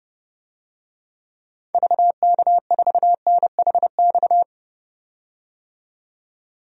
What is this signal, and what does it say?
Morse code, heard as a keyed tone, about 30 wpm, 700 Hz. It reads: VK4NHX
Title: DXCC-100 Call Signs - 6 Characters - 30wpm